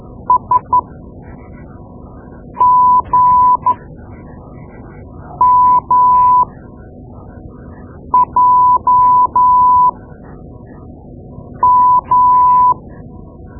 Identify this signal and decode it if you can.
Morse code, heard as a keyed tone, about 13 wpm, spaced as for 4 wpm, 1000 Hz. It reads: SGMJM
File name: Morse Lessons / Morse Lessons / Tape-2B